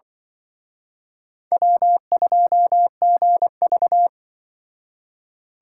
Morse code, heard as a keyed tone, about 24 wpm, 700 Hz. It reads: W2GV
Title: Speed Racing - USA Call Signs 20wpm